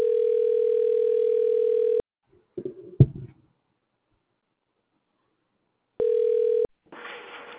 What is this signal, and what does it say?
Ringback tone (North America), 440 + 480 Hz.